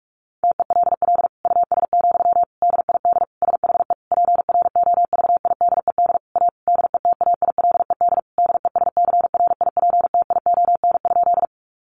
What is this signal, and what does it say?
Morse code, heard as a keyed tone, about 44 wpm, 700 Hz. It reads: TELL US, DID SHE PROVIDED A DETAILED DESCRIPTION?
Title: Sentences from Top 1500 Words 45wpm